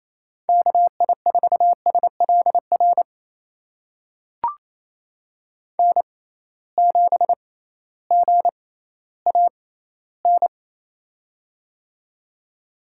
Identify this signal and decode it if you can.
Morse code, heard as a keyed tone, about 28 wpm, spaced as for 15 wpm, 700 Hz. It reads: KI4SLR N7GAN
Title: USA Call Signs - 6x Character Spacing - 28wpm